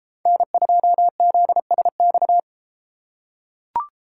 Morse code, heard as a keyed tone, about 31 wpm, 700 Hz. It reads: N2ZSX